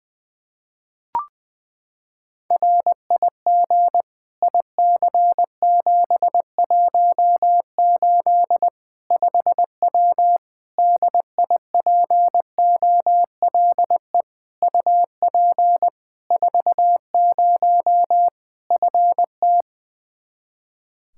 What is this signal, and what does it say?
Morse code, heard as a keyed tone, about 20 wpm, 700 Hz. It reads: RIG IC718 5W DIPOLE UP 40 FT
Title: K1USN SST QSO Practice 20wpm